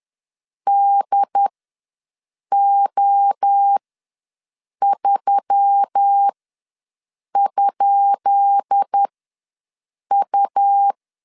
Morse code, heard as a keyed tone, about 10 wpm, spaced as for 6 wpm, 800 Hz. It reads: DO3?U
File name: LEZIONI / 10 SIGNS / 10B INT ws 10wpm 4134